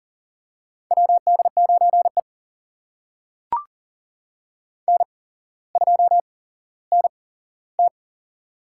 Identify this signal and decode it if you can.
Morse code, heard as a keyed tone, about 40 wpm, spaced as for 17 wpm, 700 Hz. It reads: WD9E N2NT
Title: USA Call Signs - 4 Characters - 8x Character Spacing - 40wpm